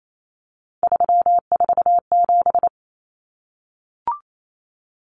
Morse code, heard as a keyed tone, about 28 wpm, 700 Hz. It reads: 347
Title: Numbers - 3-Digit 28wpm